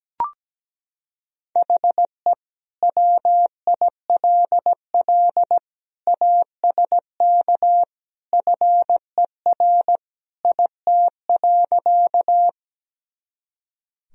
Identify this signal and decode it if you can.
Morse code, heard as a keyed tone, about 17 wpm, 700 Hz. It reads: HE WILL ASK FER IT.
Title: Sentences from Top 200 Words and Abbreviations 17wpm